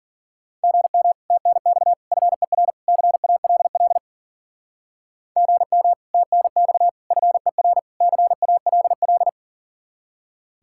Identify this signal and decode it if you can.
Morse code, heard as a keyed tone, about 43 wpm, 700 Hz. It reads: GM TNX FER CALL GM TNX FER CALL
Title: Speed-Racing Common QSO Phrases Encore - 40wpm